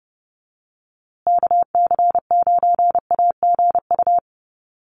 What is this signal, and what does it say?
Morse code, heard as a keyed tone, about 30 wpm, 700 Hz. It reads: KC9AGU
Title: USA Call Signs Repeated 2x 30wpm